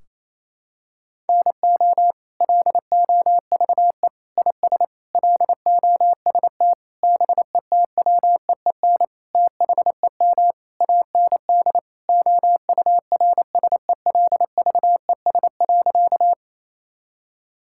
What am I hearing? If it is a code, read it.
Morse code, heard as a keyed tone, about 28 wpm, 700 Hz. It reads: NO LOVE IS LOST BETWEEN THEM AND OURSELVES.
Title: Sentences from Top 1600 Words 28wpm